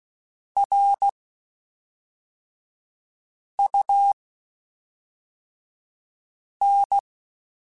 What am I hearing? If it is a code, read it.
Morse code, heard as a keyed tone, about 16 wpm, spaced as for 3 wpm, 800 Hz. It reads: RUN
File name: (spaced) / QSOs Spaced-mp3s / kc7sac @5 sp